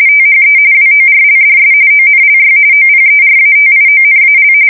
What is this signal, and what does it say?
RTTY, about 45.45 baud. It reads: THE WORLDS ONLY OPERATIONAL
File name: fsk170 / battleshipnj